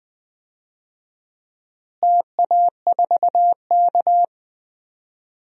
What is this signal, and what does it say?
Morse code, heard as a keyed tone, about 20 wpm, 700 Hz. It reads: TA4K
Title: DXCC-100 Call Signs - 4 Characters - Repeated 2x - 20wpm